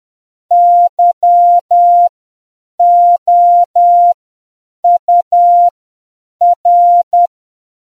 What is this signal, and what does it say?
Morse code, heard as a keyed tone, about 10 wpm, spaced as for 7 wpm, 700 Hz. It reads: YOUR